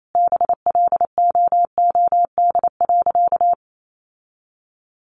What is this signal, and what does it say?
Morse code, heard as a keyed tone, about 28 wpm, 700 Hz. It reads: BLOOD.
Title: Sentences from Top 1200 Words 28wpm